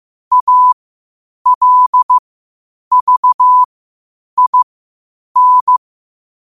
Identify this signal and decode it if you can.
Morse code, heard as a keyed tone, about 15 wpm, spaced as for 8 wpm, 1000 Hz. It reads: ALVIN
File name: qotdfiles / 8wpm